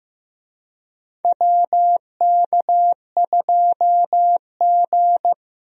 Morse code, heard as a keyed tone, about 15 wpm, 700 Hz. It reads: WK2G
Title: USA Call Signs - 4 Characters - Repeated 3x - 15wpm